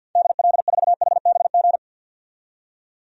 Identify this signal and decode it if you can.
Morse code, heard as a keyed tone, about 46 wpm, 700 Hz. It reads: DL4HBG
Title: DXCC-100 Call Signs - 6 Characters - 50wpm